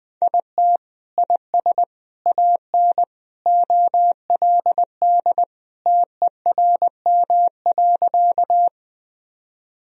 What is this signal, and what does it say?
Morse code, heard as a keyed tone, about 21 wpm, 700 Hz. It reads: IT IS AN OLD TERM.
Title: Sentences from Top 1000 Words - Review Part 4 - 20wpm